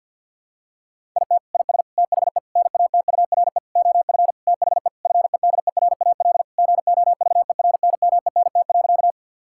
Morse code, heard as a keyed tone, about 52 wpm, 700 Hz. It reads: IT IS THE NATURE OF THE FEDERAL GOVERNMENT.
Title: Sentences from Top 1000 Words - Review Part 5 - 50wpm